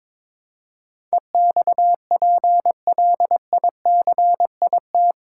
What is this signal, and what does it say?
Morse code, heard as a keyed tone, about 21 wpm, 700 Hz. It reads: EXPLICIT
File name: practice / 22wpm / Eight Letter Words - 22wpm